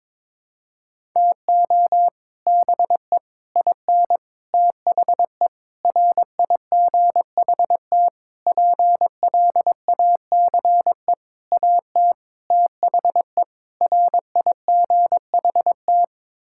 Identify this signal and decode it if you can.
Morse code, heard as a keyed tone, about 23 wpm, 700 Hz. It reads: TO BE IN THE RIGHT PLACE AT THE RIGHT
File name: Sets of 9 Words - 22wpm